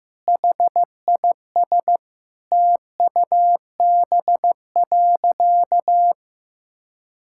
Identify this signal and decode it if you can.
Morse code, heard as a keyed tone, about 14 wpm, 700 Hz. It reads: HIS TUB.
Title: Pangram Challenge v2 - 15wpm